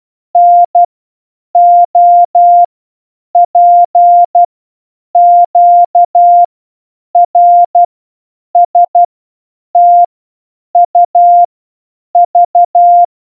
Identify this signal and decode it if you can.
Morse code, heard as a keyed tone, about 12 wpm, 700 Hz. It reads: N O P Q R S T U V
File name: A through Z in Morse code